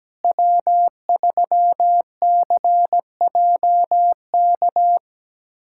Morse code, heard as a keyed tone, about 17 wpm, 700 Hz. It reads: W3CJK
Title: USA Callsign DE Callsign - 5 Characters - 17wpm